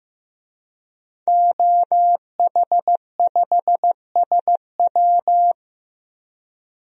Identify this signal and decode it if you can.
Morse code, heard as a keyed tone, about 15 wpm, 700 Hz. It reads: OH5SW
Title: DXCC-100 Call Signs - 5 Characters - 15wpm